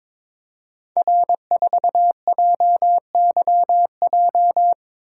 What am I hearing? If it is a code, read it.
Morse code, heard as a keyed tone, about 22 wpm, 700 Hz. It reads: R4JYJ
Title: Code Groups - 5 Letter:Numbers - 22wpm